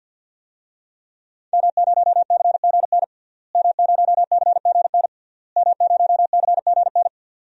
Morse code, heard as a keyed tone, about 51 wpm, 700 Hz. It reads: M0CGN M0CGN M0CGN
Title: DXCC-100 Call Signs - 5 Characters - Repeated 3x - 50wpm